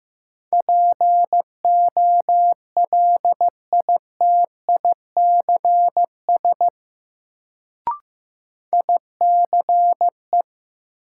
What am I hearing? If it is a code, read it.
Morse code, heard as a keyed tone, about 15 wpm, 700 Hz. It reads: POLITICS ICE
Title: Sentences from Top 1000 Words - New Word Drill - 15wpm